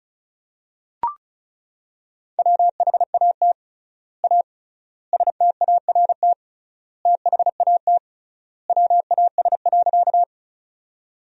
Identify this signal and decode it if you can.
Morse code, heard as a keyed tone, about 34 wpm, 700 Hz. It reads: WHAT A START THAT WAS.
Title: 3x Word-Spacing - Sentences from Top 300 Words - 35wpm